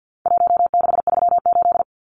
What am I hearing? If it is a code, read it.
Morse code, heard as a keyed tone, about 50 wpm, 700 Hz. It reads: 1638
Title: Numbers - Contest Style - 4-Digit 50wpm